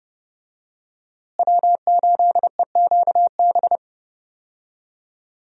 Morse code, heard as a keyed tone, about 30 wpm, 700 Hz. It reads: W8EQB
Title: USA Call Signs - 5 Characters - Repeated 2x - 30wpm